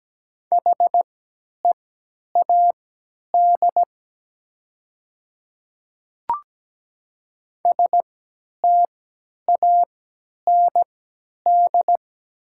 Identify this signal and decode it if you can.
Morse code, heard as a keyed tone, about 17 wpm, spaced as for 10 wpm, 700 Hz. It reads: HEAD STAND
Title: 3x Character Spacing - Top 500 Words - 17wpm